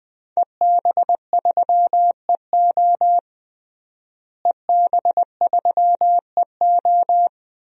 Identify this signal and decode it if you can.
Morse code, heard as a keyed tone, about 20 wpm, 700 Hz. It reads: EB3EO EB3EO
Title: DXCC-100 Call Signs - 5 Characters - Repeated 2x - 20wpm